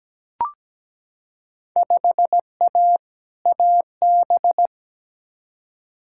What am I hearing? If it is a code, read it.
Morse code, heard as a keyed tone, about 17 wpm, 700 Hz. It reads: 5A AB
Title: Field Day Exchanges Encore - 17wpm